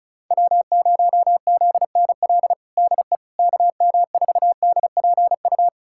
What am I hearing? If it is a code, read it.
Morse code, heard as a keyed tone, about 36 wpm, 690 Hz. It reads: W0ZNL DE KM4DPU
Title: USA Callsign DE Callsign 35wpm